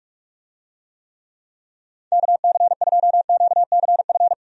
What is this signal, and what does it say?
Morse code, heard as a keyed tone, about 46 wpm, 700 Hz. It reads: KC2QCF
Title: USA Call Signs - 6 Characters - 45wpm